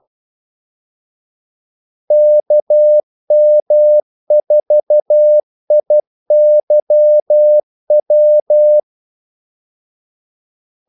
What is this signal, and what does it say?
Morse code, heard as a keyed tone, about 12 wpm, 600 Hz. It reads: KM4IYW
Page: 2 ekstra ordmellemrum mellem kaldesignalerne